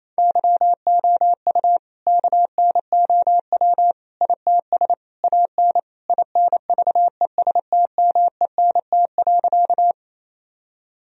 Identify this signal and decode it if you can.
Morse code, heard as a keyed tone, about 28 wpm, 700 Hz. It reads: YOU KNOW ITS AN INVESTMENT.
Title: Sentences from Top 1000 Words - Review Part 4 - 28wpm